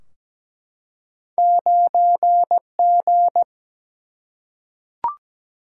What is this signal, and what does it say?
Morse code, heard as a keyed tone, about 17 wpm, 700 Hz. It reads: 9G
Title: Single-Letter Single-Number 17wpm